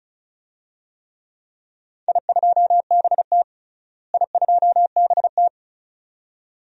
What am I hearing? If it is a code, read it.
Morse code, heard as a keyed tone, about 36 wpm, 700 Hz. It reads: I2BT I2BT
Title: DXCC-100 Call Signs - 4 Characters - Repeated 2x - 35wpm